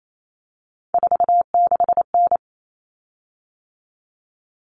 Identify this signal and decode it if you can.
Morse code, heard as a keyed tone, about 28 wpm, 700 Hz. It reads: 46N
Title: Common Cut Numbers - 3-Digit 28wpm